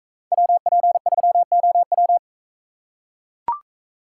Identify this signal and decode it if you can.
Morse code, heard as a keyed tone, about 41 wpm, 700 Hz. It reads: WP3OW